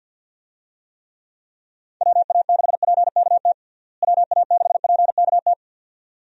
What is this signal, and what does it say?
Morse code, heard as a keyed tone, about 49 wpm, 700 Hz. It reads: WA6PKT WA6PKT
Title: USA Call Signs - 6 Characters - Repeated 2x - 50wpm